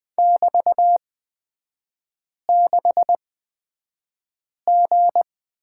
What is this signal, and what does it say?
Morse code, heard as a keyed tone, about 20 wpm, spaced as for 5 wpm, 700 Hz. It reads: <BT>6G
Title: 224 Lesson 36e - BT - Three Characters 20wpm